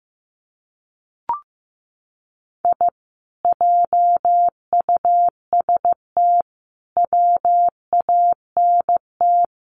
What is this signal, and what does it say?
Morse code, heard as a keyed tone, about 15 wpm, 700 Hz. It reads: I JUST WANT
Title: Sets of 4 Words 15wpm